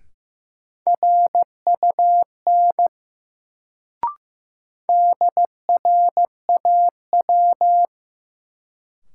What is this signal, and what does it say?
Morse code, heard as a keyed tone, about 16 wpm, 700 Hz. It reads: RUN DRAW
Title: Sentences from Top 300 Words - New Word Drill - 15wpm